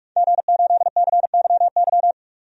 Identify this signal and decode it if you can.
Morse code, heard as a keyed tone, about 46 wpm, 700 Hz. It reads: G8CYY
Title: DXCC-100 Call Signs - 5 Characters - 45wpm